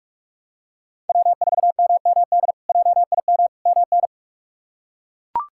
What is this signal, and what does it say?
Morse code, heard as a keyed tone, about 41 wpm, 700 Hz. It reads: W4MMD JIM MN
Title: K1USN SST QSO Practice - 45wpm